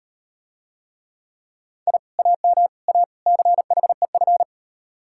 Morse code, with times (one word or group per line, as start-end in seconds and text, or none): 1.9-2.0 I
2.2-2.7 AM
2.9-3.0 A
3.3-4.4 CHEF